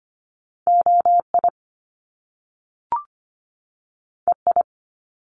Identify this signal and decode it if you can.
Morse code, heard as a keyed tone, about 25 wpm, 700 Hz. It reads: OI EI
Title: Top Two Letter Combinations in Words 25wpm